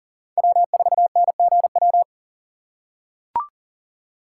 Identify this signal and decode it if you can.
Morse code, heard as a keyed tone, about 40 wpm, 700 Hz. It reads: W4NGW
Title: USA Call Signs - 3x Character Spacing - 40wpm